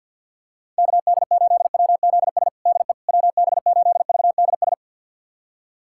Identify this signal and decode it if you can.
Morse code, heard as a keyed tone, about 52 wpm, 700 Hz. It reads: KD8WZS DE WB8VDS
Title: USA Callsign DE Callsign - 6 Characters - 50wpm